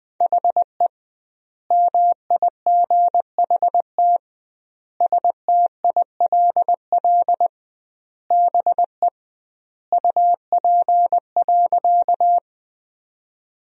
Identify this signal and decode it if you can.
Morse code, heard as a keyed tone, about 20 wpm, 700 Hz. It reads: HE MIGHT STILL BE UP.
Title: 2x Word-Spacing - Sentences from Top 300 Words - 20wpm